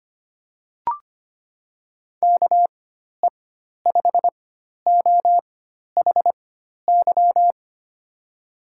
Morse code, heard as a keyed tone, about 25 wpm, spaced as for 12 wpm, 700 Hz. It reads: KE5OHY
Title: USA Call Signs - 6 Characters - 4x Character Spacing - 25wpm